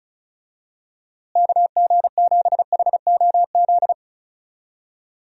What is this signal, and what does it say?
Morse code, heard as a keyed tone, about 34 wpm, 700 Hz. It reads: KG7HOZ
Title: USA Call Signs - 6 Characters - Repeated 3x - 35wpm